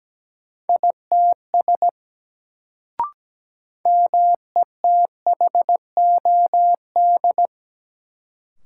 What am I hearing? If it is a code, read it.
Morse code, heard as a keyed tone, about 17 wpm, 700 Hz. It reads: ITS METHOD
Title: Sentences from Top 600 Words - New Word Drill - 17wpm